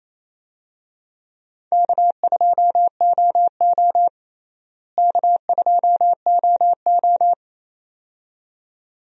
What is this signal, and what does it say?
Morse code, heard as a keyed tone, about 28 wpm, 700 Hz. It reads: K2OO K2OO
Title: USA Call Signs - 4 Characters - Repeated 2x - 28wpm